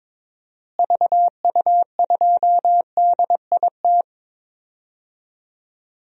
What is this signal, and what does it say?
Morse code, heard as a keyed tone, about 22 wpm, 700 Hz. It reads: VU2DIT
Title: DXCC-100 Call Signs - 6 Characters - Repeated 3x - 22wpm